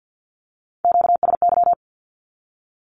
Morse code, heard as a keyed tone, about 48 wpm, 700 Hz. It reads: QSY
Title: Q-Codes 50wpm